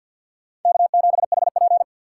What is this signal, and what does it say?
Morse code, heard as a keyed tone, about 50 wpm, 700 Hz. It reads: K7HP
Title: USA Call Signs - 4 Characters - 6x Character Spacing - 50wpm